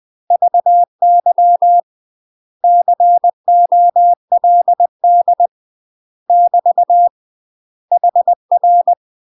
Morse code, heard as a keyed tone, about 20 wpm, 700 Hz. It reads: VY COLD <BT> HR
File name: UK MASTER QSOs TRAINING / 20WPM / UK MASTER QSOs 0045